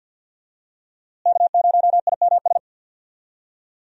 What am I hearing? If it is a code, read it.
Morse code, heard as a keyed tone, about 53 wpm, 700 Hz. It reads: K0IMS
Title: USA Call Signs - 5 Characters - Repeated 2x - 50wpm